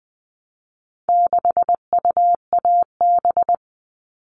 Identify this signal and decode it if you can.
Morse code, heard as a keyed tone, about 20 wpm, 700 Hz. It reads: 6UAB
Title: All Cut Numbers - 4-Digit 20wpm